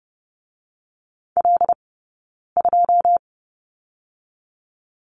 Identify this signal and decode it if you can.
Morse code, heard as a keyed tone, about 30 wpm, 700 Hz. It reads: L 2
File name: 092 Lesson 16d - Letter c - Two Characters 30wpm